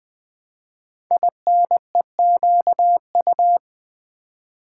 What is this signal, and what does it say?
Morse code, heard as a keyed tone, about 20 wpm, 700 Hz. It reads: INEQU
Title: Build A Word - 5 Letters At A Time - 10-Letter Words - 20wpm